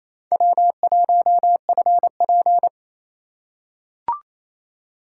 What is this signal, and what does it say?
Morse code, heard as a keyed tone, about 28 wpm, 700 Hz. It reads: W1FP
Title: USA Call Signs - 2x Character Spacing - 28wpm